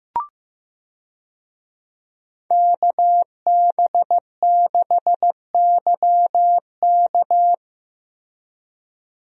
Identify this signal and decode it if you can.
Morse code, heard as a keyed tone, about 15 wpm, 700 Hz. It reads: KB6YK
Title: USA Call Signs - 5 Characters - Repeated 3x - 15wpm